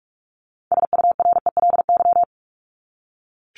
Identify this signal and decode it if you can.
Morse code, heard as a keyed tone, about 44 wpm, 700 Hz. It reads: SURELY
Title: LY Suffix - Top 100 Words with Base Words 45wpm